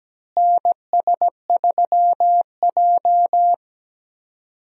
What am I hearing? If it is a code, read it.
Morse code, heard as a keyed tone, about 17 wpm, 700 Hz. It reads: NS3J
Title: USA Call Signs - 4 Characters - Repeated 3x - 17wpm